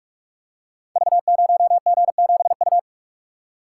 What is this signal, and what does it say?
Morse code, heard as a keyed tone, about 44 wpm, 700 Hz. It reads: V0G7U
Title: Code Groups - 5 Letter:Numbers - 45wpm